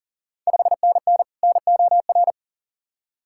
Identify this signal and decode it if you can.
Morse code, heard as a keyed tone, about 38 wpm, 690 Hz. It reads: 5NN NOR
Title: Ontario QSO Party Exchange 40wpm